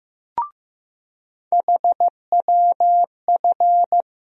Morse code, heard as a keyed tone, about 15 wpm, 700 Hz. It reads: HWF
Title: USA Call Sign Postfixes - 3 Letters 15wpm